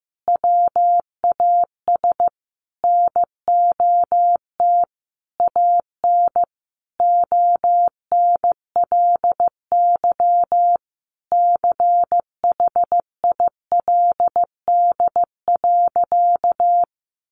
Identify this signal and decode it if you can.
Morse code, heard as a keyed tone, about 15 wpm, 700 Hz. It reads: WAS NOT AN ONLY CHILD.
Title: Sentences from Top 800 Words 15wpm